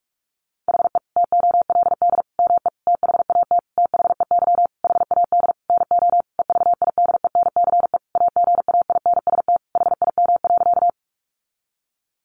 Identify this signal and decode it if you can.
Morse code, heard as a keyed tone, about 44 wpm, 700 Hz. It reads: HE TOLD ME THAT THEY HAD NO EVIDENCE AGAINST HIM.
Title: Sentences from Top 1300 Words 45wpm